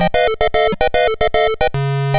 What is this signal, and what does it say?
There is no code.